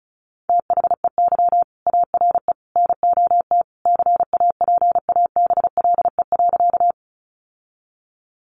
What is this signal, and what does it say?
Morse code, heard as a keyed tone, about 34 wpm, 700 Hz. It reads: THEY ARE NOT CAPABLE.
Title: Sentences from Top 1200 Words 35wpm